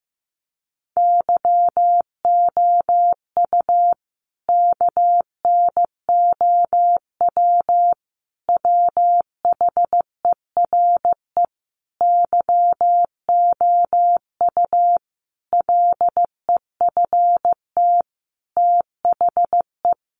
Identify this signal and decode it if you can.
Morse code, heard as a keyed tone, about 15 wpm, 700 Hz. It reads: YOU KNOW WHERE YOU LEFT THE
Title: Sentences from Top 1500 Words 15wpm